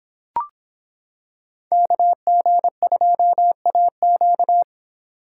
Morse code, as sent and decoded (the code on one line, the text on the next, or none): -.- --. ..--- .- --.-
KG2AQ